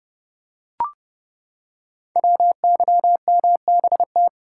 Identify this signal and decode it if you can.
Morse code, heard as a keyed tone, about 30 wpm, 690 Hz. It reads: WYMBT
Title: Code Groups - 5 Letters - 30wpm